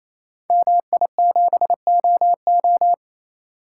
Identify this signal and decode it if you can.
Morse code, heard as a keyed tone, about 29 wpm, 700 Hz. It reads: MI7OO
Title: Code Groups - 5 Letter:Numbers - 28wpm